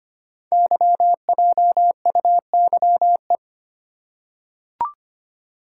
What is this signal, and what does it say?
Morse code, heard as a keyed tone, about 25 wpm, 700 Hz. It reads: YJUYE
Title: Code Groups - 5 Letters - 25wpm